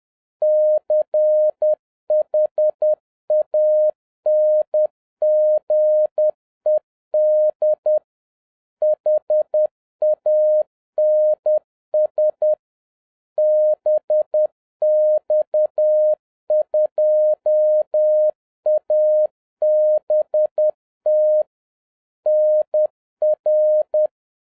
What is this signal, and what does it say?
Morse code, heard as a keyed tone, about 10 wpm, 610 Hz. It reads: CHANGED HANS BX2ABT NR